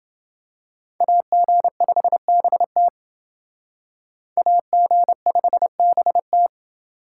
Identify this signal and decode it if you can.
Morse code, heard as a keyed tone, about 28 wpm, 700 Hz. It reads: AG5BT AG5BT